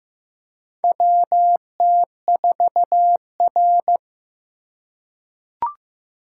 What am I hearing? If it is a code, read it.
Morse code, heard as a keyed tone, about 15 wpm, 700 Hz. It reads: WT4R
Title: USA Call Signs - 4 Characters - 3x Character Spacing - 15wpm